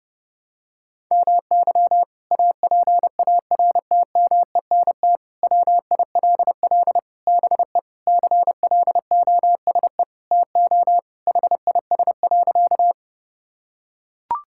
Morse code, heard as a keyed tone, about 30 wpm, 700 Hz. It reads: MY APARTMENT WILL BE CLOSE TO HIS.
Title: Sentences from Top 1000 Words - Review Part 6 - 30wpm